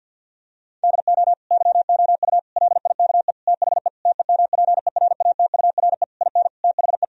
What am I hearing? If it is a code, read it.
Morse code, heard as a keyed tone, about 50 wpm, 700 Hz. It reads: DO YOU LIKE THE TEMPERATURE IN THE
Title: Sentences from Top 1000 Words - Review Part 4 - 50wpm